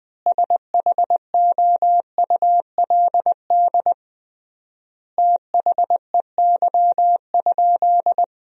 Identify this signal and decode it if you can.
Morse code, heard as a keyed tone, about 21 wpm, 700 Hz. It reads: SHOULD THEY?
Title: 3x Word-Spacing - Sentences from Top 300 Words - 20wpm